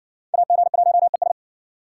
Morse code, heard as a keyed tone, about 57 wpm, 700 Hz. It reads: AG1ES